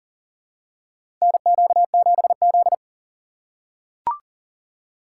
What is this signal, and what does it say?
Morse code, heard as a keyed tone, about 40 wpm, 700 Hz. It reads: NQ7Z
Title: USA Call Signs - 4 Characters - 6x Character Spacing - 40wpm